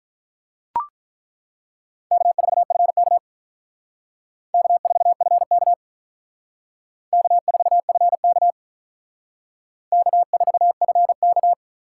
Morse code, heard as a keyed tone, about 43 wpm, 700 Hz. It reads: K4FK K4FK K4FK K4FK
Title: Speed Racing - USA 1x2 Call Signs 35wpm